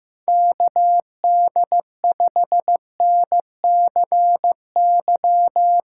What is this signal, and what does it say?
Morse code, heard as a keyed tone, about 16 wpm, 700 Hz. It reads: KD5NCY
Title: Speed Racing - USA Call Signs 15wpm